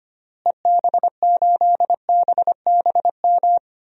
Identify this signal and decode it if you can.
Morse code, heard as a keyed tone, about 25 wpm, 700 Hz. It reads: EB8BBM
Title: DXCC-100 Call Signs - 6 Characters - Repeated 2x - 25wpm